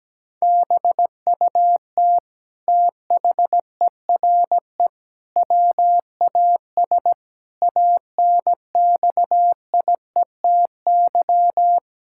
Morse code, heard as a keyed tone, about 17 wpm, 700 Hz. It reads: BUT THERE WAS ANXIETY
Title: Sentences from Top 1600 Words 17wpm